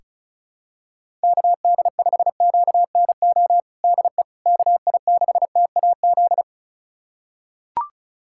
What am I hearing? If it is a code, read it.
Morse code, heard as a keyed tone, about 36 wpm, 700 Hz. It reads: KD5QNO DE KI6TAZ
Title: USA Callsign DE Callsign 35wpm